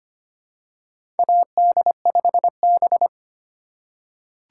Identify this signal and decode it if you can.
Morse code, heard as a keyed tone, about 25 wpm, 700 Hz. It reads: AD5B
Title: USA Call Signs - 4 Characters - 2x Character Spacing - 25wpm